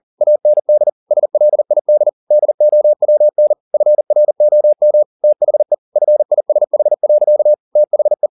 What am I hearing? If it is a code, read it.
Morse code, heard as a keyed tone, about 40 wpm, 600 Hz. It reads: AND SLID DOWN FROM THE FISH. THE
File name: the old man and the sea 2 40wpm